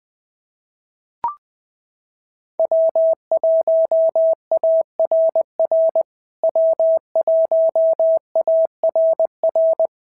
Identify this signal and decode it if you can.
Morse code, heard as a keyed tone, about 20 wpm, 650 Hz. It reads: W1ARR W1ARR
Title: POTA QSO Practice - 20wpm